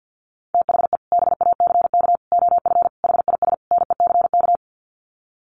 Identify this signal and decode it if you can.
Morse code, heard as a keyed tone, about 50 wpm, 700 Hz. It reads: THE BACK OF HIS NECK
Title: Sets of 5 Words Encore 50wpm